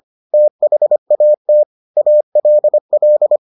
Morse code, heard as a keyed tone, about 25 wpm, 600 Hz. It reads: THAT ALL